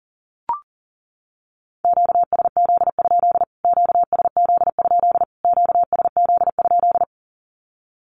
Morse code, heard as a keyed tone, about 40 wpm, 700 Hz. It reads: QSZ? QSZ? QSZ?
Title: Q-Codes Repeated 3x 40wpm